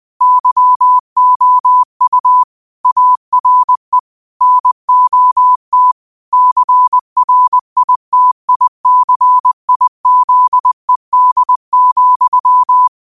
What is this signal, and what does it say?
Morse code, heard as a keyed tone, about 20 wpm, 1000 Hz. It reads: YOU ARE NOT CRITICIZED,
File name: qotdfiles / 20wpm